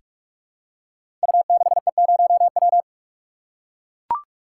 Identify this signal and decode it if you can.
Morse code, heard as a keyed tone, about 44 wpm, 700 Hz. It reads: U6E0W
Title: Code Groups - 5 Letter:Numbers - 45wpm